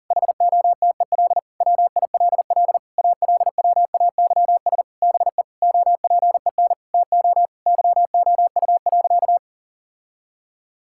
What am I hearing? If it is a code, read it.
Morse code, heard as a keyed tone, about 40 wpm, 700 Hz. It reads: HOTEL WILL ALWAYS BE OPEN TO YOU.
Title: Sentences from Top 1600 Words 40wpm